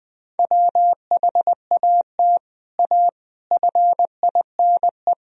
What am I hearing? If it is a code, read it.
Morse code, heard as a keyed tone, about 20 wpm, 700 Hz. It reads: WHAT A FINE
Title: Sentences from Top 1000 Words - Review Part 1 - 20wpm